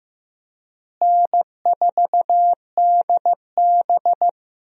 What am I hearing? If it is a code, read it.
Morse code, heard as a keyed tone, about 15 wpm, 700 Hz. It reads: N4DB
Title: USA Call Signs - 4 Characters - Repeated 2x - 15wpm